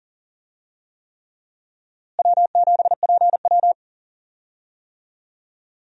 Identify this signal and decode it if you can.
Morse code, heard as a keyed tone, about 39 wpm, 700 Hz. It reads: W7PW
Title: USA Call Signs - 4 Characters - Repeated 3x - 40wpm